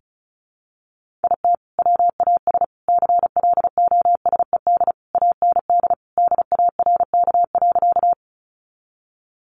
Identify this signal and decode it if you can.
Morse code, heard as a keyed tone, about 34 wpm, 700 Hz. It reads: IT WAS CLOSED AND DARK.
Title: Sentences from Top 1400 Words 35wpm